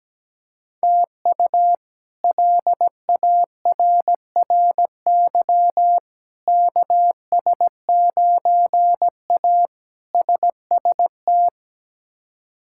Morse code, heard as a keyed tone, about 17 wpm, 700 Hz. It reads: TU LARRY KS9A SST
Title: K1USN SST QSO Practice - 17wpm